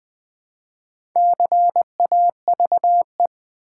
Morse code, heard as a keyed tone, about 20 wpm, 700 Hz. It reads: CAVE
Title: Build A Word - 1 Letter At A Time - 3 to 5-Letter Words - 20wpm